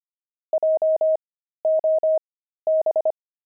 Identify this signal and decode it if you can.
Morse code, heard as a keyed tone, about 25 wpm, spaced as for 13 wpm, 630 Hz. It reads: JOB
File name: pangram-fs-13-cs-25-zeal